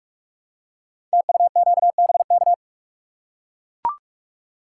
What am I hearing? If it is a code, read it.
Morse code, heard as a keyed tone, about 44 wpm, 700 Hz. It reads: TUQBK
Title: Code Groups - 5 Letters - 45wpm